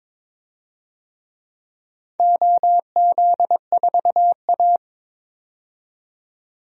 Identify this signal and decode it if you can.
Morse code, heard as a keyed tone, about 22 wpm, 700 Hz. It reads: OZ4A
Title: DXCC-100 Call Signs - 4 Characters - 22wpm